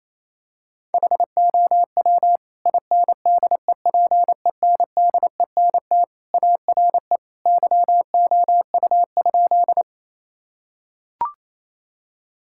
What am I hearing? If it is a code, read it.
Morse code, heard as a keyed tone, about 28 wpm, 700 Hz. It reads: HOW INDEPENDENT ARE YOU?
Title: Sentences from Top 1000 Words - Review Part 6 - 28wpm